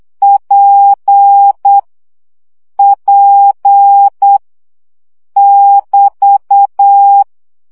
Morse code, heard as a keyed tone, about 8 wpm, 800 Hz. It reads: P P <BT>
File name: CW gyakorlas p 7 baud